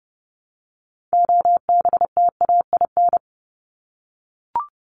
Morse code, heard as a keyed tone, about 30 wpm, 700 Hz. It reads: OBTAIN
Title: ED Suffix - Top 100 Words with Base Words 30wpm